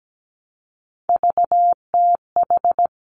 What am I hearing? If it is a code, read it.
Morse code, heard as a keyed tone, about 17 wpm, 700 Hz. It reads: VTH
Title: Three Challenging Letter-Number Combinations 17wpm